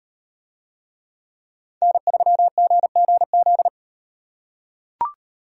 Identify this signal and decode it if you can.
Morse code, heard as a keyed tone, about 38 wpm, 700 Hz. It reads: N3GGZ